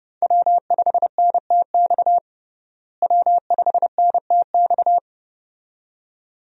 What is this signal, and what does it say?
Morse code, heard as a keyed tone, about 32 wpm, 700 Hz. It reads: W5NTX W5NTX
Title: USA Call Signs - 5 Characters - Repeated 2x - 30wpm